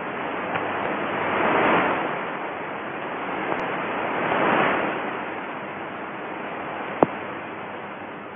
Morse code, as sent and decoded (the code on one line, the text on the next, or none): none